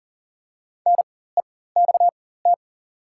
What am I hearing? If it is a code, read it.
Morse code, heard as a keyed tone, about 38 wpm, spaced as for 19 wpm, 690 Hz. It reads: NEXT
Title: 4x Character Spacing - Top 300 Words - 40wpm